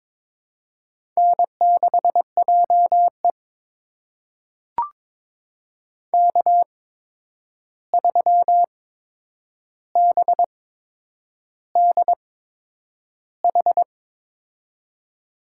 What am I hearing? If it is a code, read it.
Morse code, heard as a keyed tone, about 22 wpm, spaced as for 9 wpm, 700 Hz. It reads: N6JE K3BDH
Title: USA Call Signs - 8x Character Spacing - 22wpm